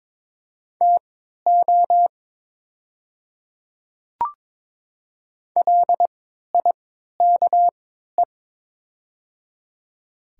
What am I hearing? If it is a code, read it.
Morse code, heard as a keyed tone, about 22 wpm, spaced as for 13 wpm, 700 Hz. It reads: TO LIKE
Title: 3x Character Spacing - Top 200 Words - 22wpm